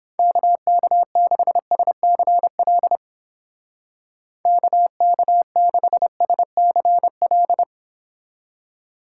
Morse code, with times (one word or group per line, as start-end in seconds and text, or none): 0.2-3.0 KK6SCL
4.5-7.6 KK6SCL